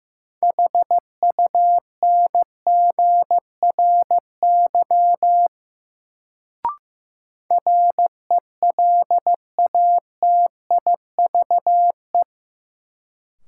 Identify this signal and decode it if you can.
Morse code, heard as a keyed tone, about 15 wpm, 700 Hz. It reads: HUNGRY RELATIVE
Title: Sentences from Top 1400 Words - New Word Drill Alt - 15wpm